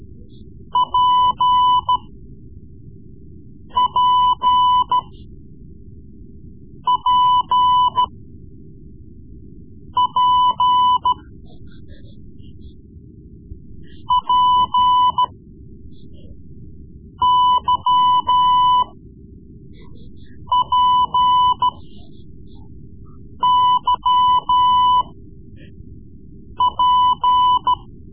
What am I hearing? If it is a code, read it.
Morse code, heard as a keyed tone, about 14 wpm, spaced as for 4 wpm, 1000 Hz. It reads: PPP P P YPYP
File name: Morse Lessons / Morse Lessons / Tape-3B